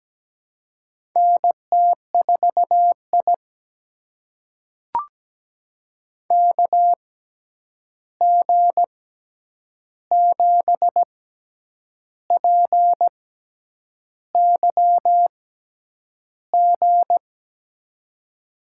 Morse code, heard as a keyed tone, about 17 wpm, spaced as for 8 wpm, 700 Hz. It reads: NT4I KG7PYG
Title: USA Call Signs - 6x Character Spacing - 17wpm